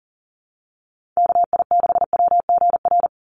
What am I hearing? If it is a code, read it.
Morse code, heard as a keyed tone, about 40 wpm, 700 Hz. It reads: KI6WGR
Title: USA Call Signs 40wpm